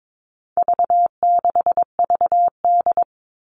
Morse code, heard as a keyed tone, about 22 wpm, 700 Hz. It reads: V6VD
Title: All Cut Numbers - 4-Digit 22wpm